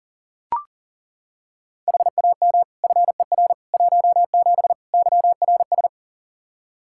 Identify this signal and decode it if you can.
Morse code, heard as a keyed tone, about 37 wpm, 690 Hz. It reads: HAM FER 17 YRS
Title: Practice QSO - Exchange 3 - More Info - 40wpm